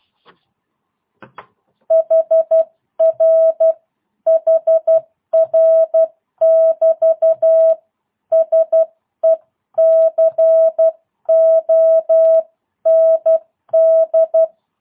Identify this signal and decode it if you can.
Morse code, heard as a keyed tone, about 12 wpm, 650 Hz. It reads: HR HR <BT> SECOND
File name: jh8cbh / 2022cw